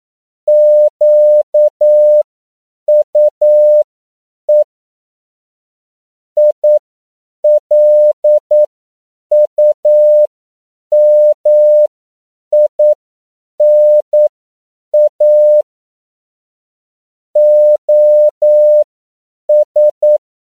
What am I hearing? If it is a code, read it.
Morse code, heard as a keyed tone, about 9 wpm, spaced as for 7 wpm, 600 Hz. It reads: QUE ILUMINA OS
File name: CW / 10PPM